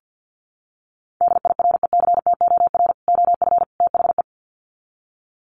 Morse code, heard as a keyed tone, about 50 wpm, 700 Hz. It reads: DIRECTOR OF THE
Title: Sets of 3 Words 50wpm